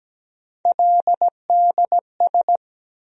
Morse code, heard as a keyed tone, about 17 wpm, 700 Hz. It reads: LDS
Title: USA Call Sign Postfixes - 3 Letters 17wpm